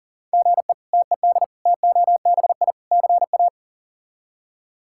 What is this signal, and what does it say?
Morse code, heard as a keyed tone, about 40 wpm, 700 Hz. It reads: GE TED TOBI CA
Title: K1USN SST QSO Practice - 40wpm